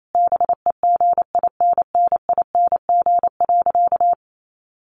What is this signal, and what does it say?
Morse code, heard as a keyed tone, about 28 wpm, 700 Hz. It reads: BEGINNING.
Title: A Sound Of Thunder 28wpm